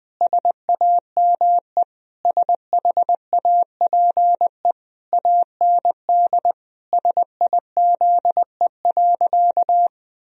Morse code, heard as a keyed tone, about 20 wpm, 700 Hz. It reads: SAME SHAPE AND SIZE.
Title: Sentences from Top 1000 Words - Review Part 3 - 20wpm